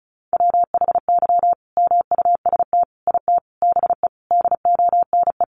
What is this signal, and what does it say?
Morse code, heard as a keyed tone, about 34 wpm, 700 Hz. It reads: WHY MUST IT BE DONE
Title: Sentences from Top 600 Words 35wpm